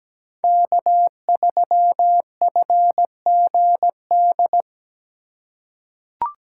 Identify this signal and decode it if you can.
Morse code, heard as a keyed tone, about 17 wpm, 700 Hz. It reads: K3FGD
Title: USA Call Signs - 5 Characters - 4x Character Spacing - 17wpm